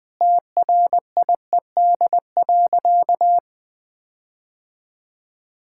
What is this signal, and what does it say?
Morse code, heard as a keyed tone, about 21 wpm, 700 Hz. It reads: TRIED.
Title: Sentences with 6-Word Phrases - Spoken Beginning - 20wpm